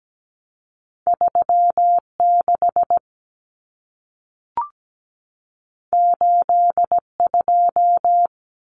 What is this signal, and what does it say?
Morse code, heard as a keyed tone, about 17 wpm, 700 Hz. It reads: 36 82
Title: Common Cut Numbers - 2-Digit 17wpm